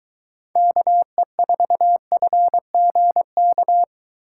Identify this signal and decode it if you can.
Morse code, heard as a keyed tone, about 23 wpm, 700 Hz. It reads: KE4FGK